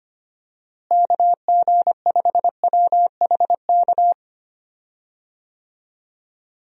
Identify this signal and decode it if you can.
Morse code, heard as a keyed tone, about 25 wpm, 700 Hz. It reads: KG5WHK
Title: USA Call Signs - 6 Characters - Repeated 2x - 25wpm